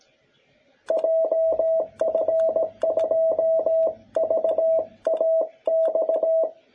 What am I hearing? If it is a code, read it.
Morse code, heard as a keyed tone, about 17 wpm, 660 Hz. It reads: JF2VAX
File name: jh8cbh / 2021cw